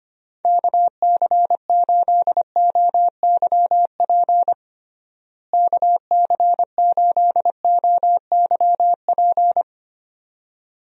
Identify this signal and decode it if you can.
Morse code, heard as a keyed tone, about 25 wpm, 700 Hz. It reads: KC8OYP KC8OYP
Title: USA Call Signs - 6 Characters - Repeated 2x - 25wpm